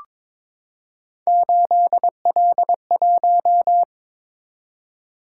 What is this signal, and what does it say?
Morse code, heard as a keyed tone, about 22 wpm, 700 Hz. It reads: 8L1
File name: practice / 22wpm / Three Letter-Number Combinations - 22wpm